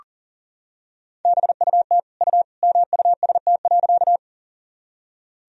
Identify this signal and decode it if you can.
Morse code, heard as a keyed tone, about 38 wpm, 700 Hz. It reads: BUT U MUST.
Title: Sentences from Top 200 Words and Abbreviations 40wpm